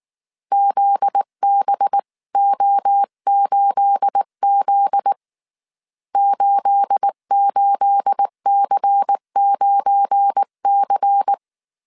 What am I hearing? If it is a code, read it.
Morse code, heard as a keyed tone, about 19 wpm, 800 Hz. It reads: ZBO8Z 88C9C
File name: CW / DET / 1 9 1500C S B 18wpm 2551